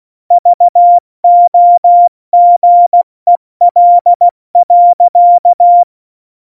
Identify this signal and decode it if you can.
Morse code, heard as a keyed tone, about 16 wpm, 700 Hz. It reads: VOGEL.
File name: Morsekurs / Witze / Witz 14. Januar 2020, 70 F80